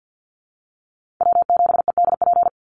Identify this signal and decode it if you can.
Morse code, heard as a keyed tone, about 50 wpm, 700 Hz. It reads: W7EDP
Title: USA Call Signs 50wpm